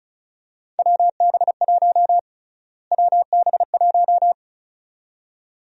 Morse code, heard as a keyed tone, about 36 wpm, 700 Hz. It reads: WB1 WB1
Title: USA Call Sign Prefixes - Repeated 2x - 35wpm